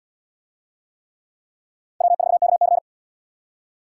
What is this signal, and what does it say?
Morse code, heard as a keyed tone, about 69 wpm, 700 Hz. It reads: W4KC